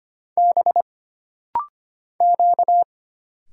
Morse code, heard as a keyed tone, about 25 wpm, 700 Hz. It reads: B Q
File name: Single Character (Letter:Number:Punctuation:Prosign) - Rapid-Fire - 25wpm